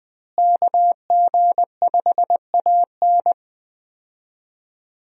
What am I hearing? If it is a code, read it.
Morse code, heard as a keyed tone, about 20 wpm, 700 Hz. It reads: KG5AN
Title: USA Call Signs - 5 Characters - Repeated 2x - 20wpm